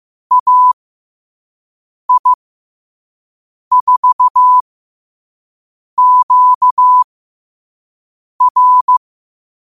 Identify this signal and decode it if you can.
Morse code, heard as a keyed tone, about 15 wpm, spaced as for 5 wpm, 1000 Hz. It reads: AI4QR